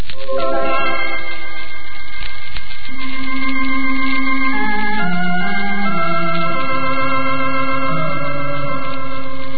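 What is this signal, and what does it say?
Dial-up modem handshake.
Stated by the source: Shortwave off-air audio received via SDRs, mostly located in Japan.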